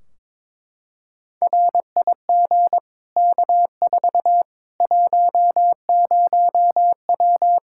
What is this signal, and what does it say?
Morse code, heard as a keyed tone, about 22 wpm, 700 Hz. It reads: RIG K4 10W
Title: Practice QSO - Exchange 2 - Info - 22wpm